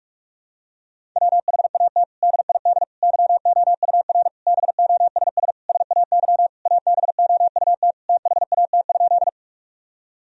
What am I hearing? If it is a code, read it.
Morse code, heard as a keyed tone, about 42 wpm, 700 Hz. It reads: WHAT DID YOUR BOSS SAY ABOUT THAT?
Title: Sentences from Top 1000 Words - Review Part 4 - 45wpm